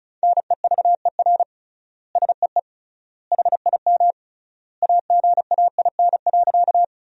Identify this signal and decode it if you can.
Morse code, heard as a keyed tone, about 36 wpm, 700 Hz. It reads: NEVER SEE HIM AGAIN.
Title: 3x Word-Spacing - Sentences from Top 300 Words - 35wpm